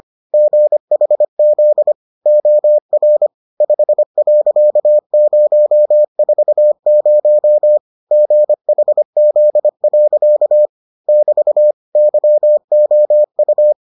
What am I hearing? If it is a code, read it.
Morse code, heard as a keyed tone, about 25 wpm, 600 Hz. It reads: GHZ OR 5.040 GHZ. <BT> YOU